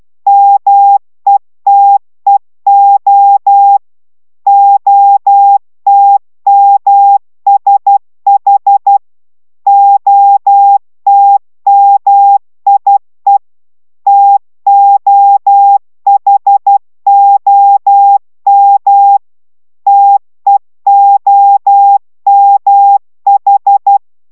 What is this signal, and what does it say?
Morse code, heard as a keyed tone, about 12 wpm, 800 Hz. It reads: METEO OTMSH OTMIE TOHOM TEOMH